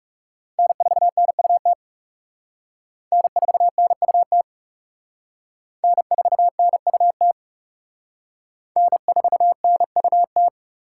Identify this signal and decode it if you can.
Morse code, heard as a keyed tone, about 36 wpm, 700 Hz. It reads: N4NUT N4NUT N4NUT N4NUT
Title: Speed Racing - USA 1x3 Call Signs 30wpm